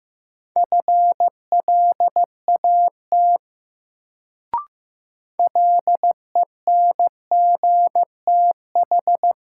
Morse code, heard as a keyed tone, about 15 wpm, 700 Hz. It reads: FLAT LENGTH
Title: Sentences from Top 1000 Words - New Word Drill - 15wpm